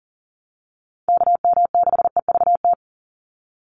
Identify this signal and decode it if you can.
Morse code, heard as a keyed tone, about 40 wpm, 700 Hz. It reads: KM6EVT
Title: USA Call Signs 40wpm